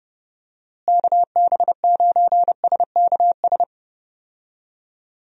Morse code, heard as a keyed tone, about 30 wpm, 700 Hz. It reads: KB9SKS
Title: USA Call Signs - 6 Characters - Repeated 2x - 30wpm